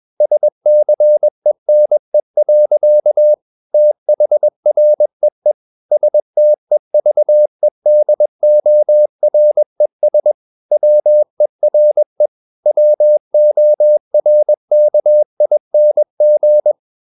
Morse code, heard as a keyed tone, about 21 wpm, 600 Hz. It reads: SCENE. THREE STEVEDORES WERE WORKING
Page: CW med Gnister